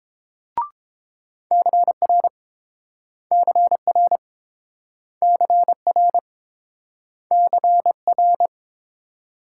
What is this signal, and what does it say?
Morse code, heard as a keyed tone, about 29 wpm, 700 Hz. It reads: CR CR CR CR